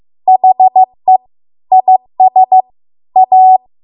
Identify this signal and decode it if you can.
Morse code, heard as a keyed tone, about 15 wpm, 750 Hz. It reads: HE IS A